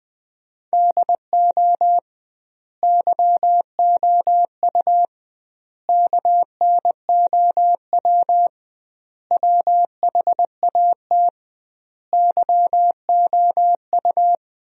Morse code, heard as a keyed tone, about 21 wpm, 700 Hz. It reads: DO YOU KNOW WHAT YOU
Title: 2x Word-Spacing - Sentences from Top 100 Words - 20wpm